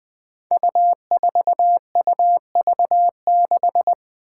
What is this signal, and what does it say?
Morse code, heard as a keyed tone, about 21 wpm, 700 Hz. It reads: U4UV6
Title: Five Challenging Letter-Number Combinations 20wpm